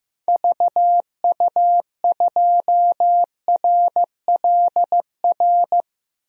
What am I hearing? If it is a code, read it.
Morse code, heard as a keyed tone, about 14 wpm, 700 Hz. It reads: VU2RLR
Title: DXCC-100 Call Signs - 6 Characters - Repeated 3x - 15wpm